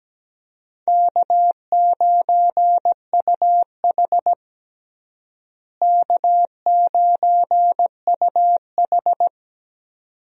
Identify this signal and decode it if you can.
Morse code, heard as a keyed tone, about 17 wpm, 700 Hz. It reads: K9UH K9UH
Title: USA Call Signs - 4 Characters - Repeated 2x - 17wpm